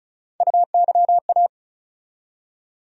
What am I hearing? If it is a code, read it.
Morse code, heard as a keyed tone, about 34 wpm, 700 Hz. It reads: UYA
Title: USA Call Sign Postfixes - 3 Letters 35wpm